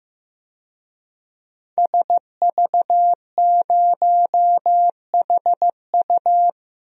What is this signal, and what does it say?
Morse code, heard as a keyed tone, about 15 wpm, 700 Hz. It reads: SV0HU
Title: DXCC-100 Call Signs - 5 Characters - 15wpm